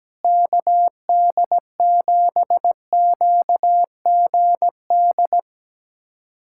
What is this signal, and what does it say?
Morse code, heard as a keyed tone, about 17 wpm, 700 Hz. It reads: KD7QGD
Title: USA Callsign DE Callsign - 6 Characters - 17wpm